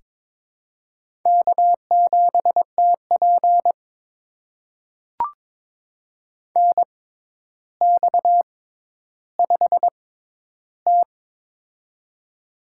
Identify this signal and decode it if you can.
Morse code, heard as a keyed tone, about 23 wpm, spaced as for 11 wpm, 700 Hz. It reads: K7TP NX5T
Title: USA Call Signs - 4 Characters - 6x Character Spacing - 22wpm